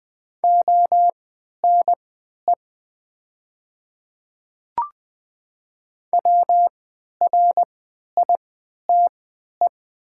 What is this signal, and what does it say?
Morse code, heard as a keyed tone, about 20 wpm, spaced as for 11 wpm, 700 Hz. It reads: ONE WRITE
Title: 3x Character Spacing - Top 500 Words - 20wpm